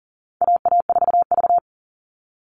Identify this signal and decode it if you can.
Morse code, heard as a keyed tone, about 38 wpm, 700 Hz. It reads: AA4V
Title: All Cut Numbers - 4-Digit 40wpm